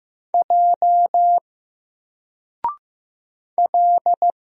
Morse code, heard as a keyed tone, about 15 wpm, 700 Hz. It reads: J L
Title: Single Character (Letter:Number:Punctuation:Prosign) 15wpm